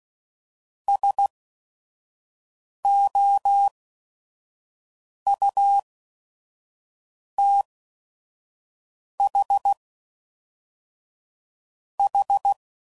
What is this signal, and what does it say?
Morse code, heard as a keyed tone, about 16 wpm, spaced as for 5 wpm, 800 Hz. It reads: SOUTHH
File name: CW Code Groups @5WPM / fromSM's / Cities